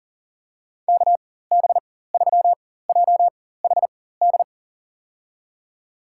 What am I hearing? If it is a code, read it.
Morse code, heard as a keyed tone, about 40 wpm, spaced as for 19 wpm, 700 Hz. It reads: KB3JHD
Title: USA Call Signs - 6 Characters - 4x Character Spacing - 40wpm